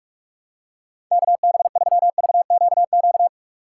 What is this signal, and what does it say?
Morse code, heard as a keyed tone, about 46 wpm, 700 Hz. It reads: KB3VQQ
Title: USA Call Signs - 2x Character Spacing - 45wpm